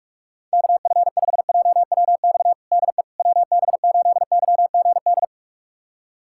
Morse code, heard as a keyed tone, about 46 wpm, 700 Hz. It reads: KF5JWX DE WB8YGD
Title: USA Callsign DE Callsign 45wpm